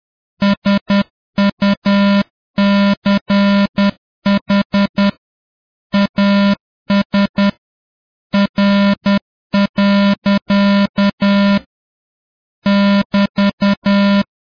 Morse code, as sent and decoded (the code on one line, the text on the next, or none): ... ..- -.-. .... / .- ... / .-. .-.-.- / -...-
SUCH AS R. <BT>